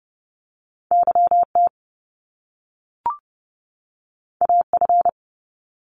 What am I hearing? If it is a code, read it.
Morse code, heard as a keyed tone, about 30 wpm, 700 Hz. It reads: YT AF
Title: Top Two Letter Combinations in Words 30wpm